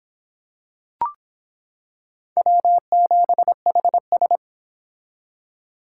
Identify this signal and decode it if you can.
Morse code, heard as a keyed tone, about 25 wpm, 700 Hz. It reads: W7HS